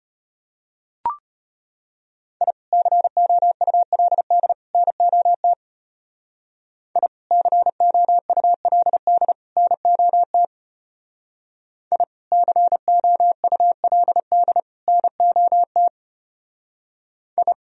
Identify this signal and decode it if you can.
Morse code, heard as a keyed tone, about 32 wpm, 700 Hz. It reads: I COULD NOT I COULD NOT I COULD NOT I